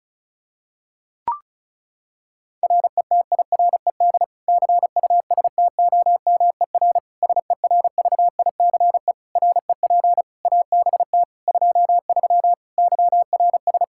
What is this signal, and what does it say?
Morse code, heard as a keyed tone, about 36 wpm, 700 Hz. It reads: RETIRED CUSTOMER SERVICE REP ABT 23 YRS
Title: K1USN SST QSO Practice 35wpm